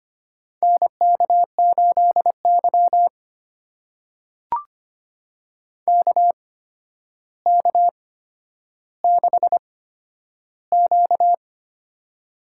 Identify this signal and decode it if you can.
Morse code, heard as a keyed tone, about 25 wpm, spaced as for 11 wpm, 700 Hz. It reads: NK8Y KK6Q
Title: USA Call Signs - 4 Characters - 8x Character Spacing - 25wpm